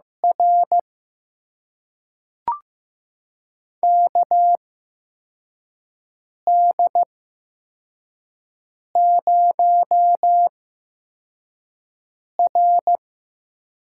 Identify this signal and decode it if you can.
Morse code, heard as a keyed tone, about 15 wpm, spaced as for 4 wpm, 700 Hz. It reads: R KD0R
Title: USA Call Signs - 8x Character Spacing - 15wpm